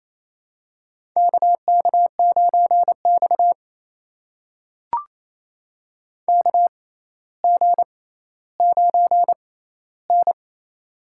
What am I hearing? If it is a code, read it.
Morse code, heard as a keyed tone, about 28 wpm, spaced as for 14 wpm, 700 Hz. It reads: KK9X KG9N
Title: USA Call Signs - 4 Characters - 6x Character Spacing - 28wpm